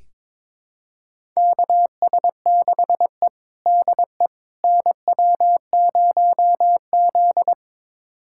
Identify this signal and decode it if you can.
Morse code, heard as a keyed tone, about 21 wpm, 700 Hz. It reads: KS6E DE NW0Z
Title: USA Callsign DE Callsign - 4 Characters - 22wpm